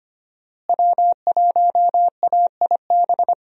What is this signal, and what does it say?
Morse code, heard as a keyed tone, about 25 wpm, 700 Hz. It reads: W1AIB
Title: USA Call Signs - 5 Characters - Repeated 2x - 25wpm